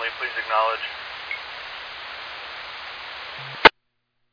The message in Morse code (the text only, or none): none